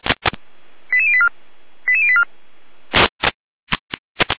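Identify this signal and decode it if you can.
Five-tone selective call (ZVEI), heard as quick runs of five tones, tones of about 70 ms. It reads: 88084 88084
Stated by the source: ZVEI five-tone call 88086 received by SDR.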